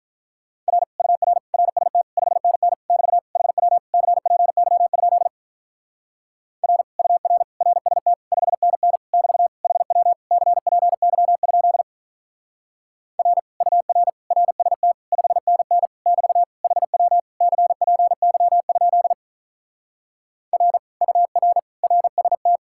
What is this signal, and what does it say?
Morse code, heard as a keyed tone, about 46 wpm, 700 Hz. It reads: R UR RST 5NN <BT> HW CPY? R UR RST 5NN <BT> HW CPY? R UR RST 5NN <BT> HW CPY? R UR RST